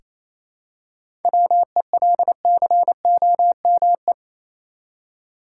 Morse code, heard as a keyed tone, about 28 wpm, 700 Hz. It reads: WELCOME
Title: Seven Letter Words - 28wpm